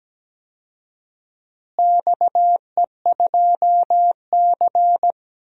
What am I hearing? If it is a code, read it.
Morse code, heard as a keyed tone, about 17 wpm, 700 Hz. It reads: XE2C
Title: DXCC-100 Call Signs - 4 Characters - Repeated 3x - 17wpm